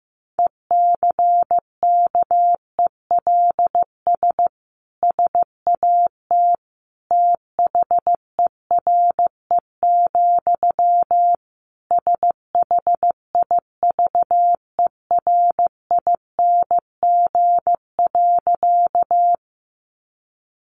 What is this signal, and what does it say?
Morse code, heard as a keyed tone, about 15 wpm, 700 Hz. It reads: ECKELS SAT THERE, SHIVERING.
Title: A Sound Of Thunder 15wpm